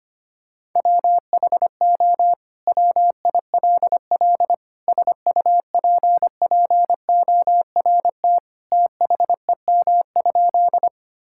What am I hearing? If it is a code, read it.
Morse code, heard as a keyed tone, about 25 wpm, 700 Hz. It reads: WHO WILL SUPPORT THEM?
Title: Sentences from Top 1000 Words - Review Part 5 - 25wpm